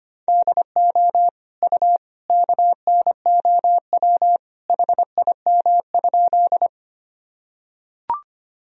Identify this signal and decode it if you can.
Morse code, heard as a keyed tone, about 25 wpm, 700 Hz. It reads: DO U KNOW HIM?
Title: Sentences from Top 100 Words and Abbreviations 25wpm